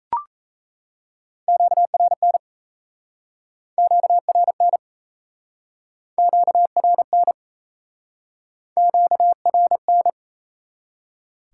Morse code, heard as a keyed tone, about 37 wpm, 700 Hz. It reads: QRN QRN QRN QRN